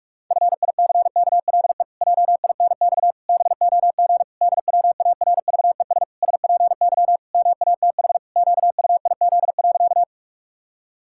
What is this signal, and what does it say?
Morse code, heard as a keyed tone, about 46 wpm, 700 Hz. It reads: FICKLE JINX BOG DWARVES SPY MATH QUIZ.
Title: Pangram Challenge v2 - 45wpm